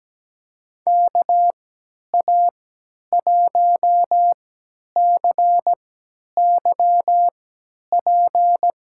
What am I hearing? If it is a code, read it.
Morse code, heard as a keyed tone, about 17 wpm, spaced as for 10 wpm, 700 Hz. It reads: KA1CYP
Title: USA Call Signs - 3x Character Spacing - 17wpm